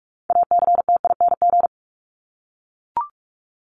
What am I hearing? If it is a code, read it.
Morse code, heard as a keyed tone, about 44 wpm, 700 Hz. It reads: ACTING
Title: ING Suffix - Top 100 Words with Base Words 45wpm